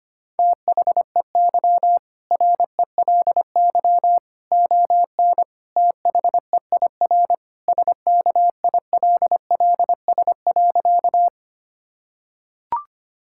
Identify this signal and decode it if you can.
Morse code, heard as a keyed tone, about 25 wpm, 700 Hz. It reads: THEY RELY ON THEIR SKILLS.
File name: practice / 25wpm / Sentences with 2-Word Phrases - 25wpm